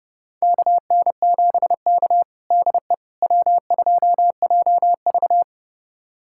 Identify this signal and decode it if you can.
Morse code, heard as a keyed tone, about 30 wpm, 700 Hz. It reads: KN7K DE W2JV
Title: USA Callsign DE Callsign - 4 Characters - 30wpm